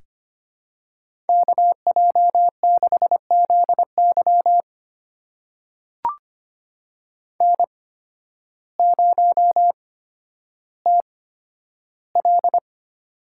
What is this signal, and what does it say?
Morse code, heard as a keyed tone, about 25 wpm, spaced as for 12 wpm, 700 Hz. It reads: KJ6ZY N0TL
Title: USA Call Signs - 8x Character Spacing - 25wpm